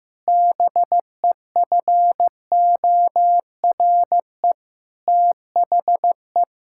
Morse code, heard as a keyed tone, about 16 wpm, 700 Hz. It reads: BEFORE THE
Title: Sentences from Top 1600 Words 15wpm